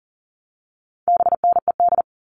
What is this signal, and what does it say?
Morse code, heard as a keyed tone, about 39 wpm, 700 Hz. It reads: BNED
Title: All Cut Numbers - 4-Digit 40wpm